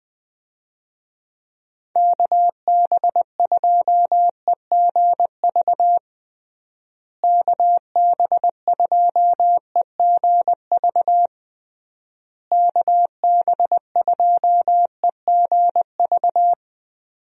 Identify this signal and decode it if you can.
Morse code, heard as a keyed tone, about 20 wpm, 700 Hz. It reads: KB2EGV KB2EGV KB2EGV
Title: USA Call Signs - 6 Characters - Repeated 3x - 20wpm